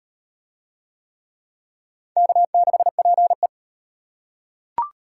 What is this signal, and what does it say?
Morse code, heard as a keyed tone, about 38 wpm, 700 Hz. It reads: K6PE